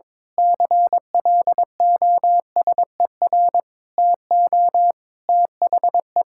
Morse code, heard as a keyed tone, about 21 wpm, 700 Hz. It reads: CLOSER TO THE
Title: Pangram Challenge v2 - 22wpm